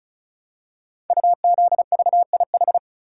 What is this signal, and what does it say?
Morse code, heard as a keyed tone, about 36 wpm, 700 Hz. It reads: UZVIH
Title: Code Groups - 5 Letters - 35wpm